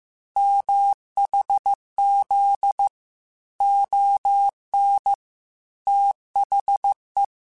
Morse code, heard as a keyed tone, about 15 wpm, 800 Hz. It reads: MHZ ON THE
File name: QSOs / QSOs-mp3s / kc7sac@13